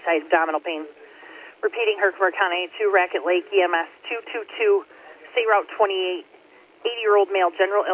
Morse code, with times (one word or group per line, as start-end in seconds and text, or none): none